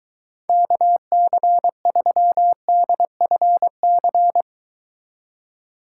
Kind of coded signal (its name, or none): Morse code